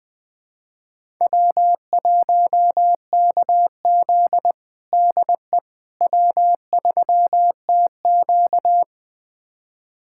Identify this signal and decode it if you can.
Morse code, heard as a keyed tone, about 20 wpm, 700 Hz. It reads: W1KZ DE W3TQ
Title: USA Callsign DE Callsign - 4 Characters - 20wpm